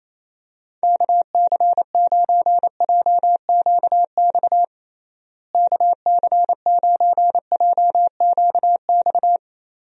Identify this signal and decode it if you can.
Morse code, heard as a keyed tone, about 28 wpm, 700 Hz. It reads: KC9JQX KC9JQX
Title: USA Call Signs - 6 Characters - Repeated 3x - 28wpm